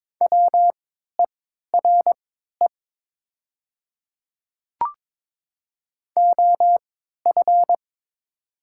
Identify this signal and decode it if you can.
Morse code, heard as a keyed tone, about 22 wpm, spaced as for 13 wpm, 700 Hz. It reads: WERE OF
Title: 3x Character Spacing - Top 200 Words - 22wpm